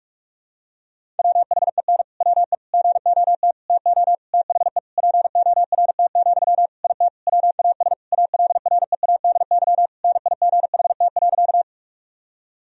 Morse code, heard as a keyed tone, about 46 wpm, 700 Hz. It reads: WHEN WE GOT TO THE PORT, IT WAS ALREADY NIGHT.
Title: Sentences from Top 1000 Words - Review Part 1 - 45wpm